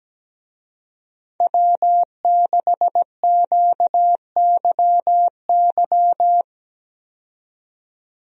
Morse code, heard as a keyed tone, about 17 wpm, 700 Hz. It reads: W6QYY
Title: USA Call Signs - 5 Characters - 17wpm